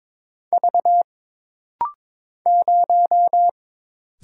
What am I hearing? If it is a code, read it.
Morse code, heard as a keyed tone, about 22 wpm, 700 Hz. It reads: V 0
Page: This rapid-fire timing is geared to push your ICR (Instant Character Recognition) skills.